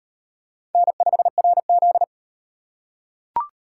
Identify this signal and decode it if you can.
Morse code, heard as a keyed tone, about 38 wpm, 700 Hz. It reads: N5RZ